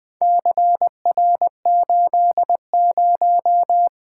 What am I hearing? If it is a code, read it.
Morse code, heard as a keyed tone, about 20 wpm, 700 Hz. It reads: CR80
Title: Four Letter-Number Combinations - 20wpm